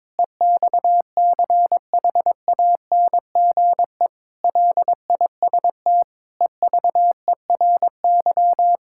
Morse code, heard as a keyed tone, about 22 wpm, 700 Hz. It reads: EXCHANGE LIST EVERY
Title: Sentences from Top 1000 Words - Review Part 6 - 22wpm